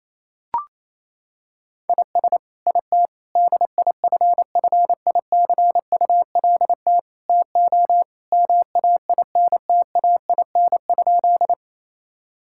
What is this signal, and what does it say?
Morse code, heard as a keyed tone, about 28 wpm, 700 Hz. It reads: IS IT DIFFICULT TO MAINTAIN?
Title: Sentences from Top 1000 Words - Review Part 5 - 28wpm